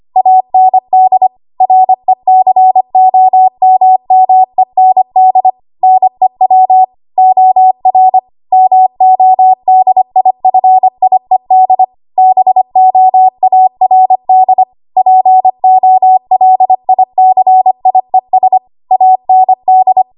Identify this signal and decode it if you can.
Morse code, heard as a keyed tone, about 25 wpm, 750 Hz. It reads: AND RECOMMEND NEW OR MODIFIED BOARD POLICIES AND
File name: Archive / 25 WPM / 25WPM